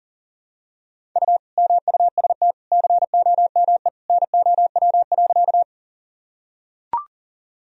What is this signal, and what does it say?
Morse code, heard as a keyed tone, about 37 wpm, 700 Hz. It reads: U MUST COME NOW.
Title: Sentences from Top 200 Words and Abbreviations 40wpm